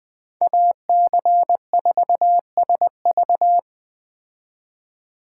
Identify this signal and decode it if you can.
Morse code, heard as a keyed tone, about 20 wpm, 700 Hz. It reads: AC4SV
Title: Speed Racing - USA 2x2 Call Signs 17wpm